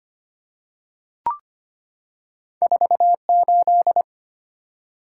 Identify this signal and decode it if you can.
Morse code, heard as a keyed tone, about 25 wpm, 700 Hz. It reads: 48
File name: Two Letter-Number Combinations - 25wpm